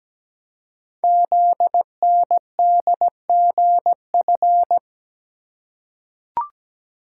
Morse code, heard as a keyed tone, about 17 wpm, 700 Hz. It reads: ZNDGF
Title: Code Groups - 5 Letters - 17wpm